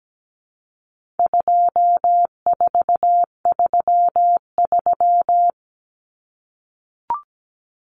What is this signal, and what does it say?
Morse code, heard as a keyed tone, about 17 wpm, 700 Hz. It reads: 2433
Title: Common Cut Numbers - 4-Digit 17wpm